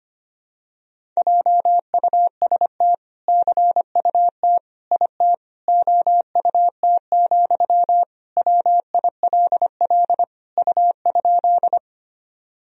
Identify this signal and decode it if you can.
Morse code, heard as a keyed tone, about 25 wpm, 700 Hz. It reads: JUST CUT IT OUT, WILL U?
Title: Sentences from Top 300 Words and Abbreviations 25wpm